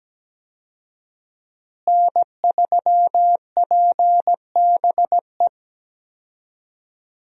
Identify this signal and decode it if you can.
Morse code, heard as a keyed tone, about 17 wpm, 700 Hz. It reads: N3PBE
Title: USA Call Signs - 5 Characters - Repeated 3x - 17wpm